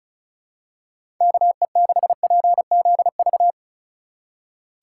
Morse code, heard as a keyed tone, about 36 wpm, 700 Hz. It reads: KE6PZV
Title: USA Call Signs - 6 Characters - 4x Character Spacing - 35wpm